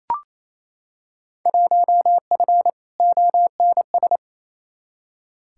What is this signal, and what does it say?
Morse code, heard as a keyed tone, about 28 wpm, 700 Hz. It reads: 1F ONS
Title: Field Day Exchanges Encore - 28wpm